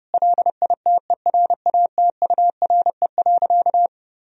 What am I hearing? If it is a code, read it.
Morse code, heard as a keyed tone, about 32 wpm, 700 Hz. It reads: LITERATURE.
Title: Sentences from Top 1000 Words - Review Part 3 - 30wpm